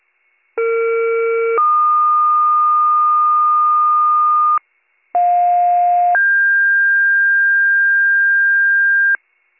Two-tone page, heard as a one-second tone, then a three-second tone, about 460 then 1220 Hz, about 710 then 1600 Hz, with noise. It is heard from 0.6 to 9.2 s.